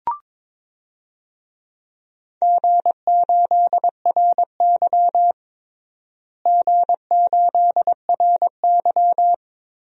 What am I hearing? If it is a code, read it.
Morse code, heard as a keyed tone, about 22 wpm, 700 Hz. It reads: G8RY G8RY
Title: DXCC-100 Call Signs - 4 Characters - Repeated 2x - 22wpm